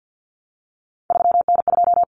Morse code, heard as a keyed tone, about 48 wpm, 700 Hz. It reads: 3N2
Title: Common Cut Numbers - 3-Digit 50wpm